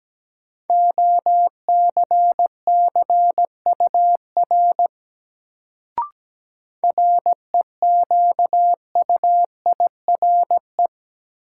Morse code, heard as a keyed tone, about 17 wpm, 700 Hz. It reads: OCCUR REQUIRE
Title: Sentences from Top 900 Words - New Word Drill - 17wpm